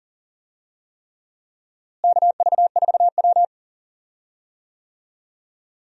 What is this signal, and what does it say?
Morse code, heard as a keyed tone, about 39 wpm, 700 Hz. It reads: KV4W
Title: USA Call Signs - 4 Characters - Repeated 3x - 40wpm